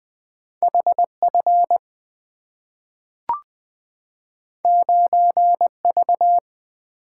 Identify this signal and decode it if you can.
Morse code, heard as a keyed tone, about 20 wpm, 690 Hz. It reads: HF 9V
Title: Two Letter-Number Combinations - 20wpm